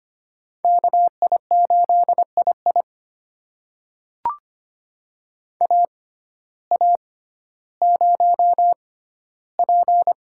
Morse code, heard as a keyed tone, about 25 wpm, spaced as for 14 wpm, 700 Hz. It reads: KI8II AA0P
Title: USA Call Signs - 5 Characters - 6x Character Spacing - 25wpm